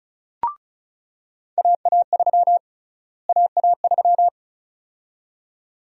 Morse code, heard as a keyed tone, about 36 wpm, 700 Hz. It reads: AA3 AA3
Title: USA Call Sign Prefixes - Repeated 2x - 35wpm